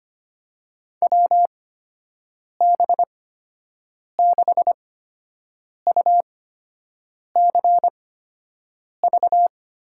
Morse code, heard as a keyed tone, about 25 wpm, spaced as for 7 wpm, 700 Hz. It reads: WB6UCV
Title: USA Call Signs - 8x Character Spacing - 25wpm